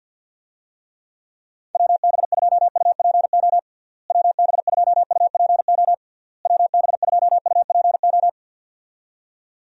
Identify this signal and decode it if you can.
Morse code, heard as a keyed tone, about 49 wpm, 700 Hz. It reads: WB2UPO WB2UPO WB2UPO
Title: USA Call Signs - 6 Characters - Repeated 3x - 50wpm